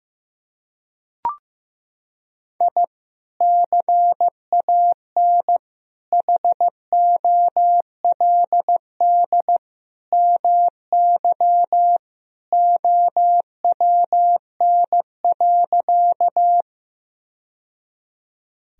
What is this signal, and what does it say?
Morse code, heard as a keyed tone, about 15 wpm, 700 Hz. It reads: I CAN HOLD MY OWN.
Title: Sentences from Top 1000 Words - Review Part 2 - 15wpm